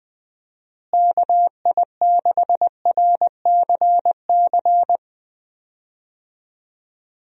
Morse code, heard as a keyed tone, about 20 wpm, 700 Hz. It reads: KI6RCC
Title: USA Call Signs - 6 Characters - Repeated 2x - 20wpm